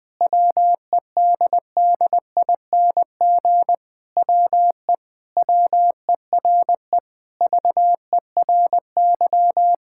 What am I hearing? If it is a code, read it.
Morse code, heard as a keyed tone, about 20 wpm, 700 Hz. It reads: WEDDING WE WERE VERY
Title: Sentences with 7-Word Phrases - 20wpm